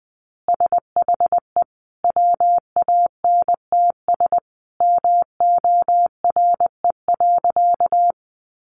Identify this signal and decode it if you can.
Morse code, heard as a keyed tone, about 20 wpm, 700 Hz. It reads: SHE WANTS MORE.
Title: Easy Phrases 20wpm